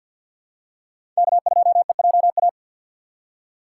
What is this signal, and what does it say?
Morse code, heard as a keyed tone, about 52 wpm, 700 Hz. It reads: K2EJA
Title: USA Call Signs - 5 Characters - 6x Character Spacing - 50wpm